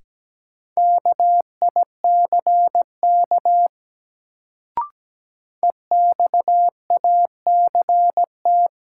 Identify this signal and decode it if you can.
Morse code, heard as a keyed tone, about 17 wpm, 700 Hz. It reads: KICK EXACT
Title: Sentences from Top 1300 Words - New Word Drill - 17wpm